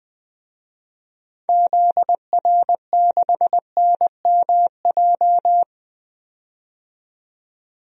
Morse code, heard as a keyed tone, about 20 wpm, 700 Hz. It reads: ZR6NMJ
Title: DXCC-100 Call Signs - 6 Characters - 20wpm